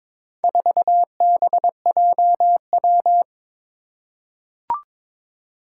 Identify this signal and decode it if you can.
Morse code, heard as a keyed tone, about 22 wpm, 700 Hz. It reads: 4BJW
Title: Four Challenging Letter-Number Combinations 22wpm